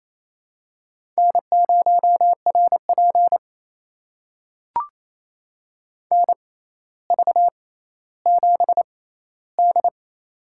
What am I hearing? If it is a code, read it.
Morse code, heard as a keyed tone, about 28 wpm, spaced as for 14 wpm, 700 Hz. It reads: N0RP NV7D
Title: USA Call Signs - 4 Characters - 6x Character Spacing - 28wpm